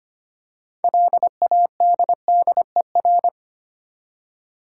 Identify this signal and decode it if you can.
Morse code, heard as a keyed tone, about 25 wpm, 690 Hz. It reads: LADDER
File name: practice / 25wpm / Easy Words with Hints - 25wpm